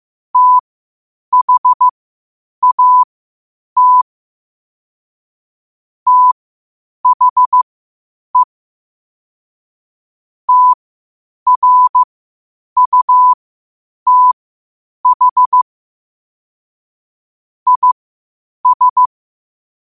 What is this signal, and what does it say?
Morse code, heard as a keyed tone, about 15 wpm, spaced as for 8 wpm, 1000 Hz. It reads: THAT THE TRUTH IS
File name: qotdfiles / 8wpm